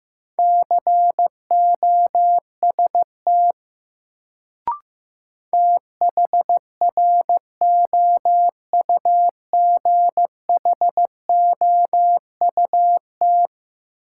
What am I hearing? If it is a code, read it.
Morse code, heard as a keyed tone, about 16 wpm, 700 Hz. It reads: COST THROUGHOUT
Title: Sentences from Top 800 Words - New Word Drill - 15wpm